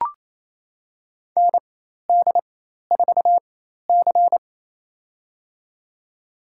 Morse code, heard as a keyed tone, about 28 wpm, spaced as for 13 wpm, 700 Hz. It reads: ND4C
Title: USA Call Signs - 4 Characters - 4x Character Spacing - 28wpm